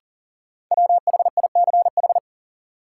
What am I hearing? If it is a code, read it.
Morse code, heard as a keyed tone, about 40 wpm, 700 Hz. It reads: WHICH
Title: Sentences from Top 100 Words - New Word Drill - 40wpm